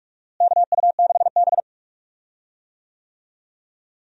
Morse code, heard as a keyed tone, about 46 wpm, 700 Hz. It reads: KU6B
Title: USA Call Signs - 4 Characters - Repeated 3x - 45wpm